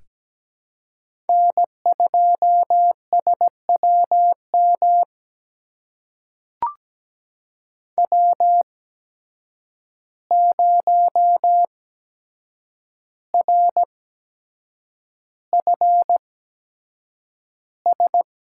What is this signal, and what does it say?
Morse code, heard as a keyed tone, about 17 wpm, spaced as for 7 wpm, 700 Hz. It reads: N2SWM W0RFS
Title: USA Call Signs - 5 Characters - 8x Character Spacing - 17wpm